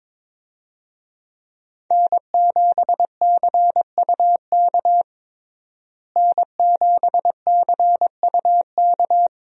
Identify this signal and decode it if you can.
Morse code, heard as a keyed tone, about 22 wpm, 700 Hz. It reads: N7CUK N7CUK
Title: USA Call Signs - 5 Characters - Repeated 2x - 22wpm